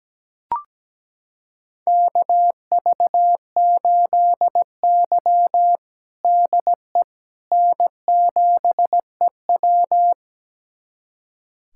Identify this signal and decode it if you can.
Morse code, heard as a keyed tone, about 17 wpm, 700 Hz. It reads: KV8Y DE N7EW
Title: USA Callsign DE Callsign - 4 Characters - 17wpm